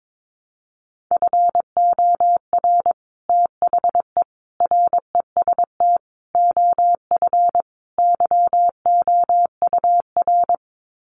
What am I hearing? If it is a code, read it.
Morse code, heard as a keyed tone, about 22 wpm, 700 Hz. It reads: FOR THE REST OF YOUR
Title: Sets of 5 Words Encore 22wpm